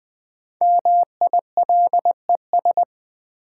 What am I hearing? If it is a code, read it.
Morse code, heard as a keyed tone, about 20 wpm, 700 Hz. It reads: MILES
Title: Sentences from Top 1000 Words - Review Part 3 - New Word Drill - 20wpm